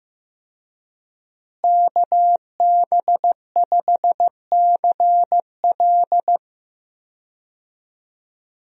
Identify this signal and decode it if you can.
Morse code, heard as a keyed tone, about 15 wpm, 700 Hz. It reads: KB5CL
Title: USA Call Signs - 5 Characters - Repeated 3x - 15wpm